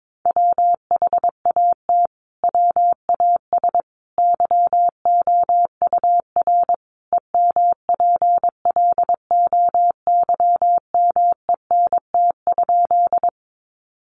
Morse code, heard as a keyed tone, about 22 wpm, 700 Hz. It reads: WHAT WAS YOUR EMPLOYMENT?
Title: Sentences from Top 1500 Words 22wpm